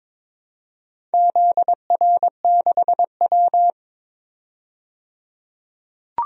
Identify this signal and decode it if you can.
Morse code, heard as a keyed tone, about 22 wpm, 700 Hz. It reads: ZR6W
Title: DXCC-100 Call Signs - 4 Characters - 22wpm